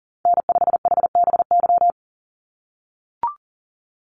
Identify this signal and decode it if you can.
Morse code, heard as a keyed tone, about 39 wpm, 700 Hz. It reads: N5HBY
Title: USA Call Signs 40wpm